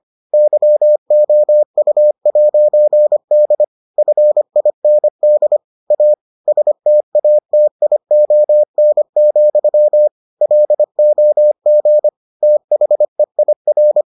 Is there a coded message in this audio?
Yes: Morse code from 0.3 to 14.0 s, heard as a keyed tone, about 25 wpm, 600 Hz. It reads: YOU'D FIND A STATION, LOG THEIR